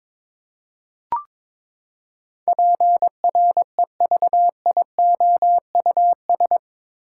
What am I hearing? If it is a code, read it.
Morse code, heard as a keyed tone, about 22 wpm, 700 Hz. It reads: PREVIOUS
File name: practice / 22wpm / Eight Letter Words - 22wpm